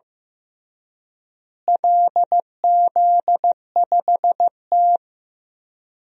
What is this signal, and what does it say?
Morse code, heard as a keyed tone, about 16 wpm, 700 Hz. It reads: LZ5T
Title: DXCC-100 Call Signs - 4 Characters - Repeated 2x - 15wpm